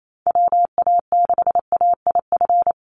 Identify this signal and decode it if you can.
Morse code, heard as a keyed tone, about 28 wpm, 700 Hz. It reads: WA6AIF
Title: USA Call Signs 28wpm